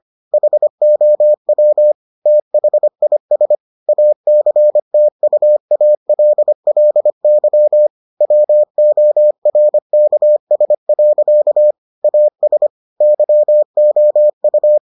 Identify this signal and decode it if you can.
Morse code, heard as a keyed tone, about 25 wpm, 600 Hz. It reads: HOW THIS ACTUALLY WORKS. AS YOU